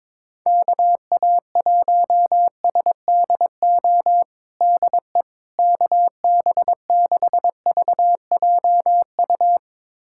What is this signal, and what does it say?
Morse code, heard as a keyed tone, about 21 wpm, 700 Hz. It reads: KA1SDO DE KB6VJU
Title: USA Callsign DE Callsign - 6 Characters - 22wpm